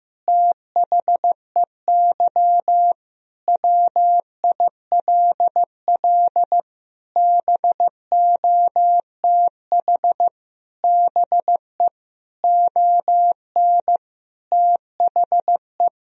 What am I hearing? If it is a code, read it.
Morse code, heard as a keyed tone, about 16 wpm, 700 Hz. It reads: THEY WILL BOTH BE ON THE
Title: Sentences from Top 1000 Words - Review Part 2 - 15wpm